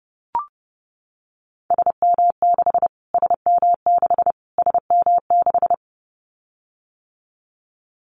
Morse code, heard as a keyed tone, about 30 wpm, 700 Hz. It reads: SM6 SM6 SM6
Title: Callsign Prefixes for Top 25 DXCC - Repeated 3x - 30wpm